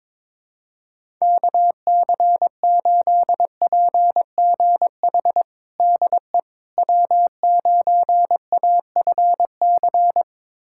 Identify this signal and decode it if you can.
Morse code, heard as a keyed tone, about 22 wpm, 700 Hz. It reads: KC8PGH DE W9AFC
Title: USA Callsign DE Callsign 22wpm